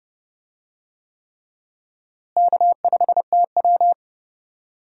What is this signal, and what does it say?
Morse code, heard as a keyed tone, about 31 wpm, 700 Hz. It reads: K5TW
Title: USA Call Signs - 4 Characters - Repeated 3x - 30wpm